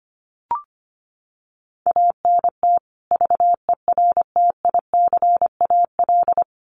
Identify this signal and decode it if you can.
Morse code, heard as a keyed tone, about 25 wpm, 700 Hz. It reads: ANT VERTICAL
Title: Common QSO Phrases 25wpm